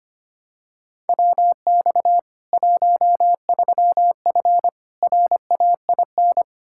Morse code, heard as a keyed tone, about 25 wpm, 700 Hz. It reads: WX 13F RAIN
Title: K1USN SST QSO Practice 25wpm